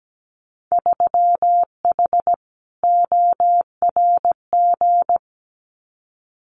Morse code, heard as a keyed tone, about 17 wpm, 700 Hz. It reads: 3H ORG
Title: Winter Field Day Exchanges Encore 17wpm